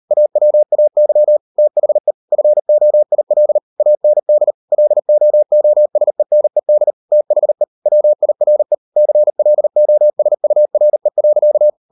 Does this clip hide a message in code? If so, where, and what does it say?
Yes: Morse code from 0.1 to 11.7 s, heard as a keyed tone, about 38 wpm, 600 Hz. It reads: AWAY THE FOIL AND LOOSENED THE WIRE CLOSURE.